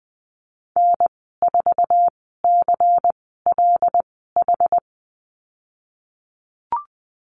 Morse code, heard as a keyed tone, about 20 wpm, spaced as for 15 wpm, 700 Hz. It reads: N4CLH
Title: 147 Lesson 24g - Letter y - Call Signs 20wpm